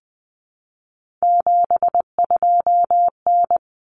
Morse code, heard as a keyed tone, about 20 wpm, 700 Hz. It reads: 72N
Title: Common Cut Numbers - 3-Digit 20wpm